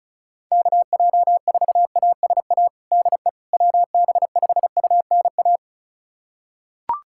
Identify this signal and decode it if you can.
Morse code, heard as a keyed tone, about 36 wpm, 700 Hz. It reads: KJ4ASA DE WB5UNA
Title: USA Callsign DE Callsign - 6 Characters - 35wpm